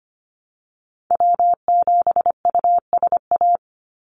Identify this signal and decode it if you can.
Morse code, heard as a keyed tone, about 25 wpm, 700 Hz. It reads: W7USA
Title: NA SOTA Chaser Callsigns 25wpm